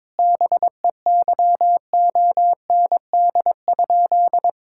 Morse code, heard as a keyed tone, about 22 wpm, 700 Hz. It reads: BEYOND?
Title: Sentences from Top 1000 Words - Review Part 6 - 22wpm